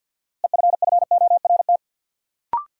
Morse code, heard as a keyed tone, about 53 wpm, 700 Hz. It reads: EFFORT
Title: Sentences from Top 1200 Words - New Word Drill - 50wpm